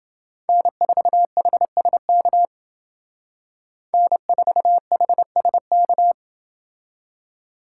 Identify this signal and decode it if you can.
Morse code, heard as a keyed tone, about 28 wpm, 700 Hz. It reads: N4HSK N4HSK